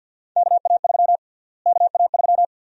Morse code, heard as a keyed tone, about 52 wpm, 700 Hz. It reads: KA3 KA3
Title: USA Call Sign Prefixes - Repeated 2x - 50wpm